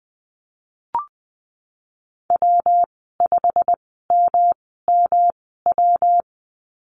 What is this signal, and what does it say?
Morse code, heard as a keyed tone, about 19 wpm, spaced as for 14 wpm, 700 Hz. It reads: W5MMW
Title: 161 Lesson 26g - Letter g - Call Signs 20wpm